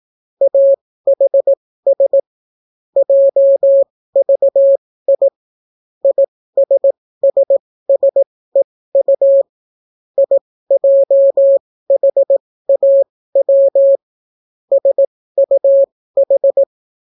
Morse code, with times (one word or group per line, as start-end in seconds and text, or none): 0.4-2.2 AHS
3.0-5.3 JVI
6.0-9.4 ISSSEU
10.2-14.0 IJHAW
14.7-16.6 SUH